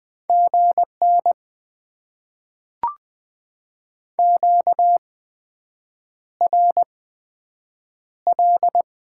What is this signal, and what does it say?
Morse code, heard as a keyed tone, about 20 wpm, spaced as for 7 wpm, 700 Hz. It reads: GN QRL
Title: Top 90 QSO Elements - 8x Character Spacing - 20wpm